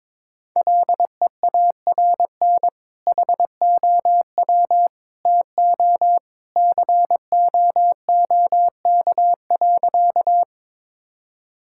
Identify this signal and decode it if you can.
Morse code, heard as a keyed tone, about 22 wpm, 700 Hz. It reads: LEARN HOW TO COOK.
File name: practice / 22wpm / Sentences with 2-Word Phrases - 22wpm